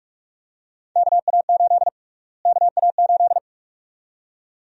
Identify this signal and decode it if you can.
Morse code, heard as a keyed tone, about 46 wpm, 700 Hz. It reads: KA8 KA8
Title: USA Call Sign Prefixes - Repeated 2x - 45wpm